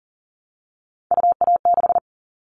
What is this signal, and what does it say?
Morse code, heard as a keyed tone, about 40 wpm, 700 Hz. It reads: UA6
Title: Callsign Prefixes for Top 50 DXCC - 40wpm